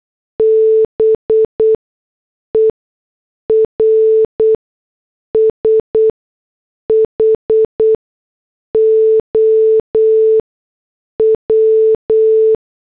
Morse code, heard as a keyed tone, about 8 wpm, spaced as for 6 wpm, 440 Hz. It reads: BER SHOW